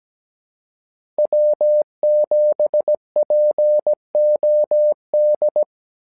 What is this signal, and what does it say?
Morse code, heard as a keyed tone, about 17 wpm, 600 Hz. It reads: W7POD
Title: SOTA QSO Practice 17wpm